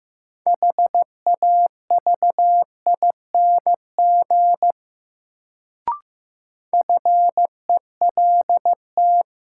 Morse code, heard as a keyed tone, about 15 wpm, 700 Hz. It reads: HAVING FELT
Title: Sets of 5 Words Encore - New Word Drill - 15wpm